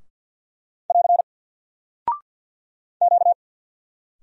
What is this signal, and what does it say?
Morse code, heard as a keyed tone, about 48 wpm, 700 Hz. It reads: <AR> Q
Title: Single Character (Letter:Number:Punctuation:Prosign) Alt 50wpm